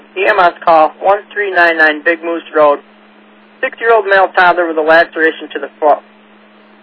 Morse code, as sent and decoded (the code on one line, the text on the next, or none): none